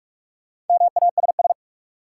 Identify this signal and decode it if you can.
Morse code, heard as a keyed tone, about 46 wpm, 700 Hz. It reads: MASS
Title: Sets of 5 Words Encore - New Word Drill - 45wpm